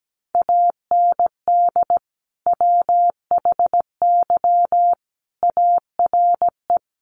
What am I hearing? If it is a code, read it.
Morse code, heard as a keyed tone, about 17 wpm, 700 Hz. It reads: AND WHY ARE
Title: Sentences from Top 1000 Words 17wpm